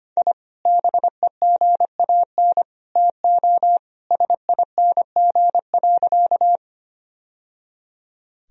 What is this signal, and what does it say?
Morse code, heard as a keyed tone, about 25 wpm, 700 Hz. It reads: I BEGAN TO SING.
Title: Sentences from Top 1000 Words - Review Part 2 - 25wpm